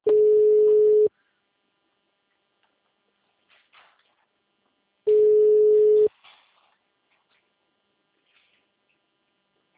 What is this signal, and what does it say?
Ringback tone (Europe, 425 Hz).
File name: out- -2012- - - .104